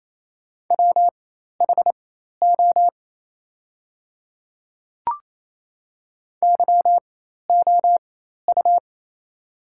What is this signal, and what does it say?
Morse code, heard as a keyed tone, about 28 wpm, spaced as for 13 wpm, 700 Hz. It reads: WHO YOU
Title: 4x Character Spacing - Top 100 Words - 28wpm